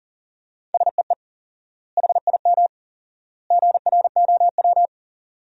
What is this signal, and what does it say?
Morse code, heard as a keyed tone, about 37 wpm, 700 Hz. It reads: SEE HIM GROW
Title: 4x Word-Spacing - Sentences from Top 300 Words - 40wpm